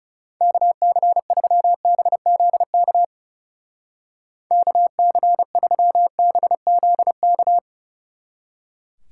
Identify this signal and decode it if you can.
Morse code, heard as a keyed tone, about 34 wpm, 700 Hz. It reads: KC3BZK KC3BZK
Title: Speed Racing - USA 2x3 Call Signs 30wpm